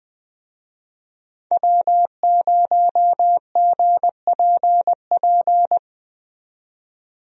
Morse code, heard as a keyed tone, about 20 wpm, 700 Hz. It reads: W0GPP
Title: USA Call Signs - 5 Characters - 20wpm